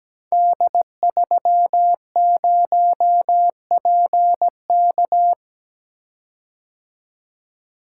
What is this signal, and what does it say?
Morse code, heard as a keyed tone, about 17 wpm, 700 Hz. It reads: D30PK
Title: Code Groups - 5 Letter:Numbers - 17wpm